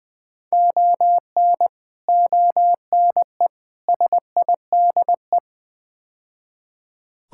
Morse code, heard as a keyed tone, about 20 wpm, 700 Hz. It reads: ON ONE SIDE
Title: Build A Phrase - 5-Word Phrases - 20wpm